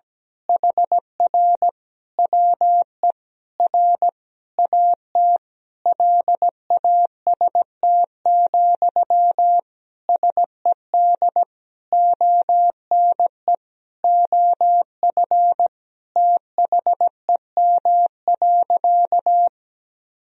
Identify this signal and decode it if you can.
Morse code, heard as a keyed tone, about 17 wpm, 700 Hz. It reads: HR WE R AT LAST, SED ONE OF THEM.
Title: Sentences from Top 300 Words and Abbreviations 17wpm